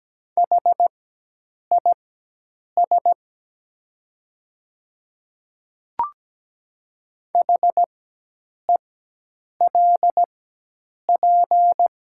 Morse code, heard as a keyed tone, about 17 wpm, spaced as for 8 wpm, 700 Hz. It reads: HIS HELP
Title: 4x Character Spacing - Top 200 Words - 17wpm